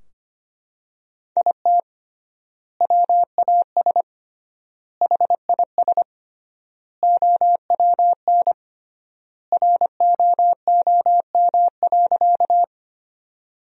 Morse code, heard as a keyed tone, about 25 wpm, 700 Hz. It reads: IT WAS HIS OWN ROOM.
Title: 3x Word-Spacing - Sentences from Top 300 Words - 25wpm